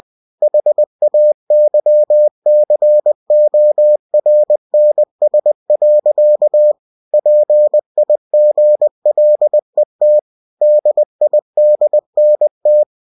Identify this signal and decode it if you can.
Morse code, heard as a keyed tone, about 20 wpm, 600 Hz. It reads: HAYCORNS. PIGLET DIDNT
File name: cw-books / winnie-the-pooh / winnie-20wpm-page-0037